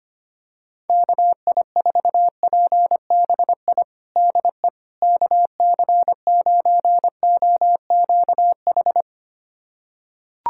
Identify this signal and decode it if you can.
Morse code, heard as a keyed tone, about 25 wpm, 700 Hz. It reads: KI4PBI DE KC9OQH
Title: USA Callsign DE Callsign 25wpm